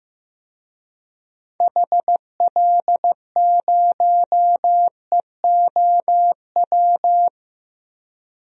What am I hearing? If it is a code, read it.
Morse code, heard as a keyed tone, about 15 wpm, 700 Hz. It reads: HL0EOW
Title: DXCC-100 Call Signs - 6 Characters - Repeated 2x - 15wpm